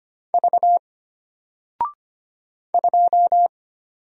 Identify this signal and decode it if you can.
Morse code, heard as a keyed tone, about 25 wpm, 700 Hz. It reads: V 2
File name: Single-Letter-Number-25wpm